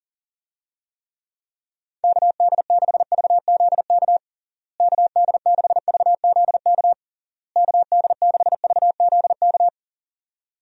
Morse code, heard as a keyed tone, about 40 wpm, 700 Hz. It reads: KD6VZK KD6VZK KD6VZK
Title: USA Call Signs - 6 Characters - Repeated 3x - 40wpm